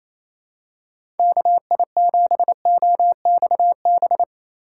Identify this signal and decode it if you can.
Morse code, heard as a keyed tone, about 28 wpm, 700 Hz. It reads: KI7OXB
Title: USA Call Signs - 6 Characters - 8x Character Spacing - 28wpm